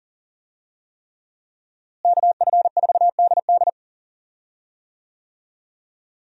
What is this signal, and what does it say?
Morse code, heard as a keyed tone, about 40 wpm, 700 Hz. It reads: KF4DD
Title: USA Call Signs - 5 Characters - Repeated 3x - 40wpm